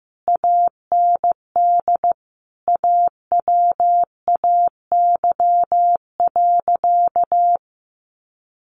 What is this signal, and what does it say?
Morse code, heard as a keyed tone, about 15 wpm, 700 Hz. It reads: AND AWAY.
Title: A Sound Of Thunder 15wpm